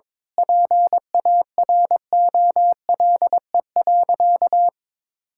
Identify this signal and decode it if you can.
Morse code, heard as a keyed tone, about 22 wpm, 700 Hz. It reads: PAROLE.
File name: practice / 22wpm / Sentences with 8-Word Phrases - 22wpm